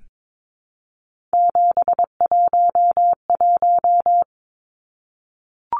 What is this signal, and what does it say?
Morse code, heard as a keyed tone, about 22 wpm, 700 Hz. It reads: 711
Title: Numbers - 3-Digit 22wpm